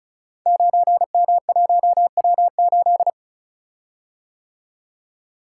Morse code, heard as a keyed tone, about 34 wpm, 690 Hz. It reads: 9M1W8
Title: Code Groups - 5 Letter:Numbers - 35wpm